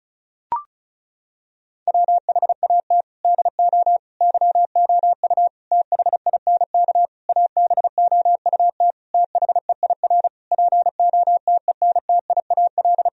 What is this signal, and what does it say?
Morse code, heard as a keyed tone, about 36 wpm, 700 Hz. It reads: WHAT DO YOU THINK ABOUT THEIR POTENTIAL
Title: Sentences from Top 1000 Words - Review Part 4 - 35wpm